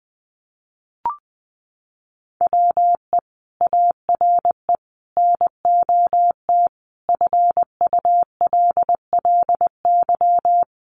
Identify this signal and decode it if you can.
Morse code, heard as a keyed tone, about 20 wpm, 700 Hz. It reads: WE ARE NOT FULLY
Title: Sentences from Top 1300 Words 20wpm